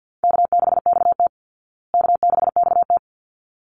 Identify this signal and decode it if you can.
Morse code, heard as a keyed tone, about 50 wpm, 700 Hz. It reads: K6XT K6XT
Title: Easy Call Signs Repeated 2x - 50wpm